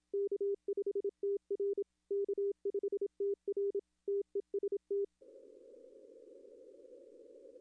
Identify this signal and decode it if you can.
Morse code, heard as a keyed tone, about 27 wpm, 400 Hz. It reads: K5TR K5TR TEST
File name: contests / arrl-160-k5tr-op-w5tn -1300